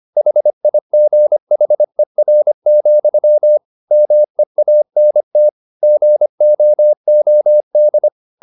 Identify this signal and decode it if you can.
Morse code, heard as a keyed tone, about 25 wpm, 600 Hz. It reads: HIGHER, MEANT GOOD
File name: the old man and the sea part 1 25wpm